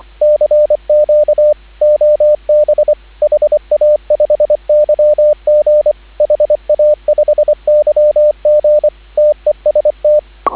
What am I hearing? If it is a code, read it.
Morse code, heard as a keyed tone, about 25 wpm, 590 Hz, with radio noise. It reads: CQ OB HA5YG HA5YG TEST